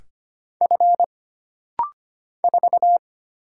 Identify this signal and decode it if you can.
Morse code, heard as a keyed tone, about 25 wpm, 700 Hz. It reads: F 4
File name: Single Character (Letter:Number:Punctuation:Prosign) - Rapid-Fire - 25wpm